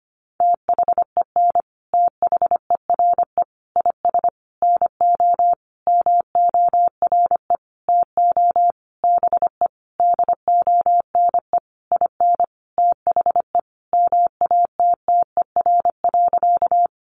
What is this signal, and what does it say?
Morse code, heard as a keyed tone, about 25 wpm, 700 Hz. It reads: THEN THERE IS NO MORE TO BE DONE IN THE MATTER.
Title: Sentences from Top 700 Words 25wpm